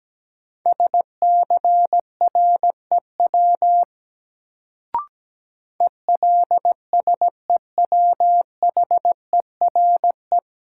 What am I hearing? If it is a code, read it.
Morse code, heard as a keyed tone, about 17 wpm, 700 Hz. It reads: SCREW ELSEWHERE
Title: Sentences from Top 1600 Words - New Word Drill - 17wpm